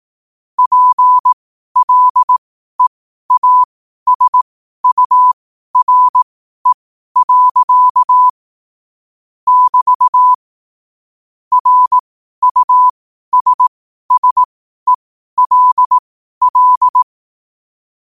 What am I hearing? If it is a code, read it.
Morse code, heard as a keyed tone, about 18 wpm, spaced as for 12 wpm, 1000 Hz. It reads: PLEASURE. <BT> RUSSELL
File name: qotdfiles / 12wpm